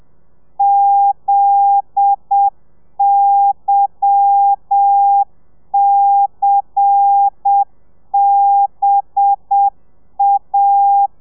Morse code, heard as a keyed tone, about 7 wpm, 800 Hz. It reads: ZYCBA
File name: telegrafia / i2tfj / Lez10 30